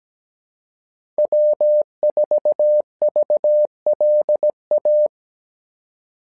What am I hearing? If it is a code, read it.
Morse code, heard as a keyed tone, about 17 wpm, 600 Hz. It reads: W4VLA
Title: SOTA QSO Practice 17wpm